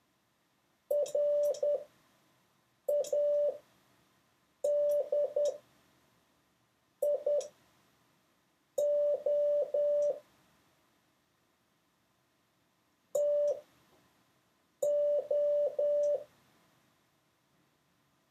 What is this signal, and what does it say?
Morse code, heard as a keyed tone, about 10 wpm, spaced as for 5 wpm, 580 Hz. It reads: RADIO TO
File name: 6of6-test